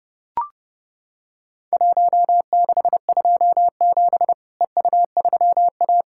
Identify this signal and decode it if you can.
Morse code, heard as a keyed tone, about 30 wpm, 700 Hz. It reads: 1627 EU3A
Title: Worked All Europe DX Contest - QTC Exchange - 30wpm